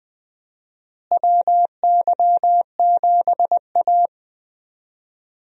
Morse code, heard as a keyed tone, about 21 wpm, 700 Hz. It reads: WY7A
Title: USA Call Signs - 4 Characters - Repeated 3x - 20wpm